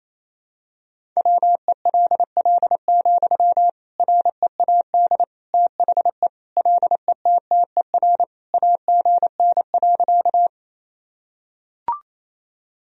Morse code, heard as a keyed tone, about 28 wpm, 700 Hz. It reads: WELL, READ THE LETTER AGN.
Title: Sentences from Top 300 Words and Abbreviations 28wpm